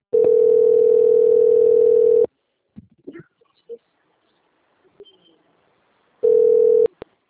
Ringback tone (North America), 440 + 480 Hz.